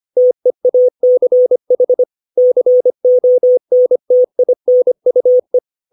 Morse code, heard as a keyed tone, about 25 wpm, 500 Hz. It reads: TEACH CONTINUE
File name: 632 words / 632 Words 25WPM